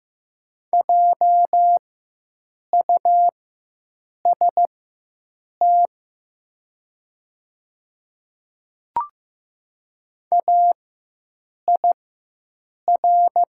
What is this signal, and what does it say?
Morse code, heard as a keyed tone, about 15 wpm, spaced as for 7 wpm, 700 Hz. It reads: JUST AIR
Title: 4x Character Spacing - Top 300 Words - 15wpm